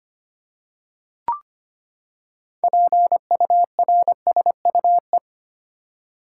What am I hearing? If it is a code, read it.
Morse code, heard as a keyed tone, about 25 wpm, 700 Hz. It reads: PURSUE
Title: Six Letter Words - 25wpm